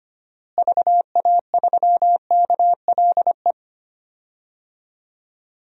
Morse code, heard as a keyed tone, about 25 wpm, 700 Hz. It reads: VA3KLE
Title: DXCC-100 Call Signs - 6 Characters - Repeated 3x - 25wpm